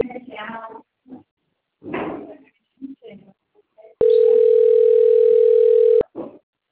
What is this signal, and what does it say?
Ringback tone (North America), 440 + 480 Hz, with speech.
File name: out- - - - - .19225